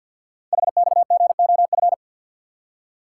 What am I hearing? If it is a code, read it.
Morse code, heard as a keyed tone, about 50 wpm, 700 Hz. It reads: HXGOF
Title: Code Groups - 5 Letters - 50wpm